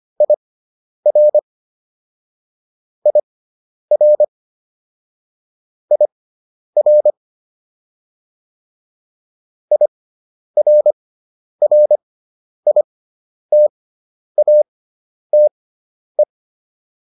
Morse code, heard as a keyed tone, about 25 wpm, spaced as for 10 wpm, 600 Hz. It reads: IR IR IR IRRITATE